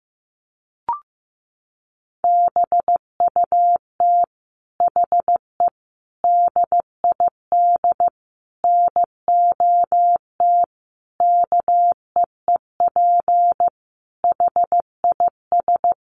Morse code, heard as a keyed tone, about 15 wpm, 700 Hz. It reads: BUT HE DID NOT KEEP HIS
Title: Sentences from Top 1500 Words 15wpm